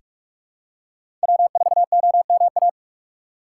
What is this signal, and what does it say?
Morse code, heard as a keyed tone, about 47 wpm, 700 Hz. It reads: W4OMA
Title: USA Call Signs - 5 Characters - 4x Character Spacing - 45wpm